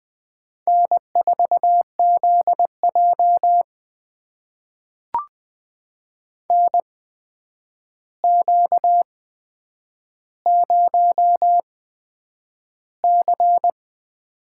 Morse code, heard as a keyed tone, about 20 wpm, spaced as for 9 wpm, 700 Hz. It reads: N4ZJ NQ0C
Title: USA Call Signs - 4 Characters - 8x Character Spacing - 20wpm